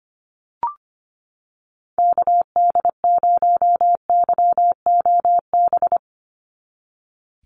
Morse code, heard as a keyed tone, about 25 wpm, 700 Hz. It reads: KD0YOB
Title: NA SOTA Activator Callsigns 25wpm